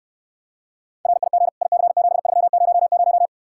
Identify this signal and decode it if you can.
Morse code, heard as a keyed tone, about 69 wpm, 700 Hz. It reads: RIG IC7300